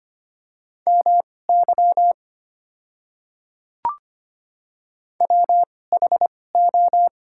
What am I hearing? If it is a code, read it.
Morse code, heard as a keyed tone, about 25 wpm, spaced as for 18 wpm, 700 Hz. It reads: MY WHO
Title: 2x Character Spacing - Top 200 Words - 25wpm